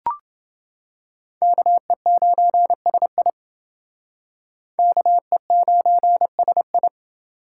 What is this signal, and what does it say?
Morse code, heard as a keyed tone, about 30 wpm, 700 Hz. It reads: KE9SI KE9SI